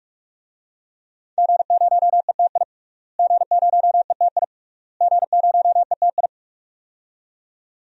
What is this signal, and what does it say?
Morse code, heard as a keyed tone, about 42 wpm, 700 Hz. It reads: G0ETI G0ETI G0ETI
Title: DXCC-100 Call Signs - 5 Characters - Repeated 3x - 45wpm